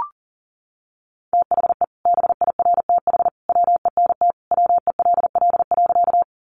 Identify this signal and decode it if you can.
Morse code, heard as a keyed tone, about 40 wpm, 700 Hz. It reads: THE BIRTH WENT WELL.
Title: Sentences from Top 800 Words 40wpm